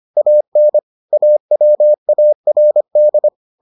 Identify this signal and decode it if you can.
Morse code, heard as a keyed tone, about 25 wpm, 600 Hz. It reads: AN AWARD